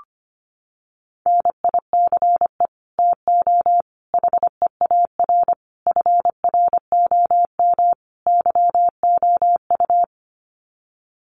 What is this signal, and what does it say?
Morse code, heard as a keyed tone, about 25 wpm, 700 Hz. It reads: NICE TO HEAR FROM YOU
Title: Common QSO Phrases 25wpm